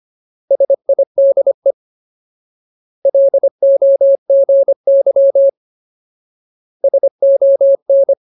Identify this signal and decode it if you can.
Morse code, heard as a keyed tone, about 25 wpm, 550 Hz. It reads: SIDE LOGY SON
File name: Ultimate Word List / 25WPM Ultimate